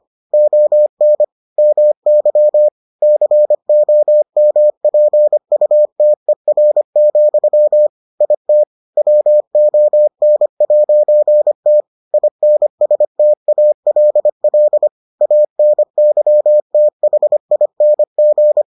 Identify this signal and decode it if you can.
Morse code, heard as a keyed tone, about 25 wpm, 600 Hz. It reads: ON MY COMPUTER, IT WON'T INSTALL ANYTHING